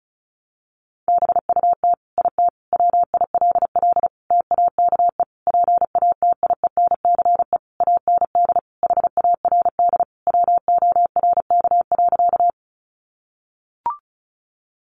Morse code, heard as a keyed tone, about 34 wpm, 700 Hz. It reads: BUT IT WILL TAKE PATIENCE AND HARD WORK.
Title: Sentences from Top 1300 Words 35wpm